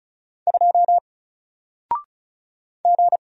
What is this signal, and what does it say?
Morse code, heard as a keyed tone, about 36 wpm, 700 Hz. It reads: 2 G
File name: Single-Letter-Number-35wpm